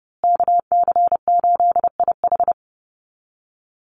Morse code, heard as a keyed tone, about 30 wpm, 700 Hz. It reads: KC8IH
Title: USA Call Signs Repeated 2x 30wpm